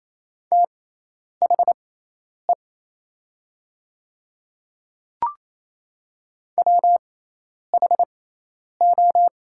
Morse code, heard as a keyed tone, about 28 wpm, spaced as for 10 wpm, 700 Hz. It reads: THE WHO
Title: 6x Character Spacing - Top 200 Words - 28wpm